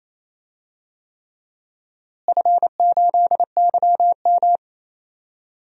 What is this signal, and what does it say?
Morse code, heard as a keyed tone, about 27 wpm, 700 Hz. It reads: F8YM
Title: DXCC-100 Call Signs - 4 Characters - Repeated 3x - 28wpm